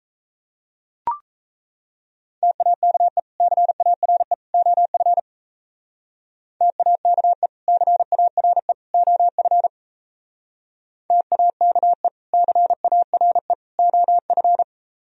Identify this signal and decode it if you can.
Morse code, heard as a keyed tone, about 36 wpm, 700 Hz. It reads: TAKE CARE OF TAKE CARE OF TAKE CARE OF